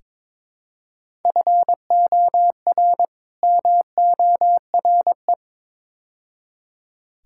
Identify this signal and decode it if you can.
Morse code, heard as a keyed tone, about 22 wpm, 700 Hz. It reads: FOR MORE
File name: practice / 22wpm / Build A Phrase - 5-Word Phrases - 22wpm